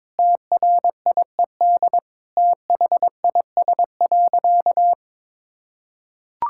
Morse code, heard as a keyed tone, about 22 wpm, 700 Hz. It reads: TRIED THIS.
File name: practice / 22wpm / Sentences with 4-Word Phrases - 22wpm